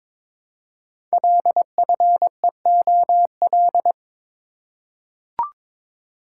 Morse code, heard as a keyed tone, about 22 wpm, 700 Hz. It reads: LFEOL
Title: Code Groups - 5 Letters - 22wpm